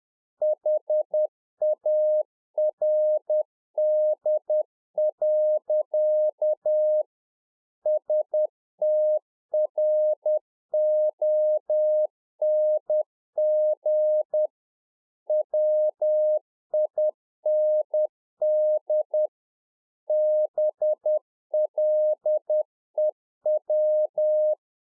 Morse code, heard as a keyed tone, about 10 wpm, 610 Hz. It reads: HARD. STRONG WIND BLEW